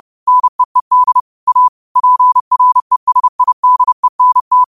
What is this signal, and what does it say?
Morse code, heard as a keyed tone, about 30 wpm, 1000 Hz. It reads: NEED A PRESIDENT
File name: qotdfiles / 30wpm